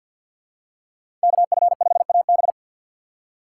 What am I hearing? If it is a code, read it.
Morse code, heard as a keyed tone, about 52 wpm, 700 Hz. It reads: KF5AB
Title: USA Call Signs - 5 Characters - 6x Character Spacing - 50wpm